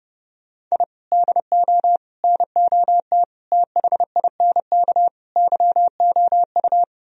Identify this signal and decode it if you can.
Morse code, heard as a keyed tone, about 39 wpm, 700 Hz. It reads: I DO NOT THINK YOU